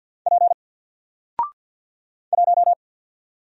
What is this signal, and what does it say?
Morse code, heard as a keyed tone, about 50 wpm, 700 Hz. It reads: P 1
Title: Single Letter-Number 50wpm